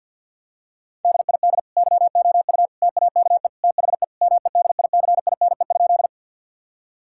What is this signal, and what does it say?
Morse code, heard as a keyed tone, about 50 wpm, 700 Hz. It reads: DID YOU TAKE THE MEDICINE?
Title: Sentences from Top 1000 Words - Review Part 6 - 50wpm